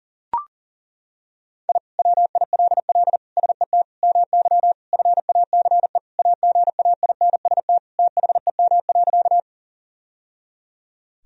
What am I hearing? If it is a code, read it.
Morse code, heard as a keyed tone, about 37 wpm, 700 Hz. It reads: I WILL SET MY FACE AGAINST THEM.
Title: Sentences from Top 1000 Words - Review Part 2 - 40wpm